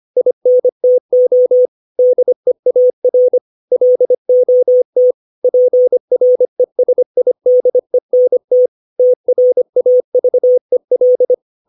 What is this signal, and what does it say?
Morse code, heard as a keyed tone, about 25 wpm, 500 Hz. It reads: INTO DEAR LOT PRESIDENT TRAVEL
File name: 632 words / 632 Words 25WPM